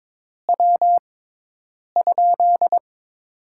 Morse code, heard as a keyed tone, about 22 wpm, spaced as for 8 wpm, 700 Hz. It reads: W?
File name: Top 20 QSO Elements - 6x Character Spacing - 22wpm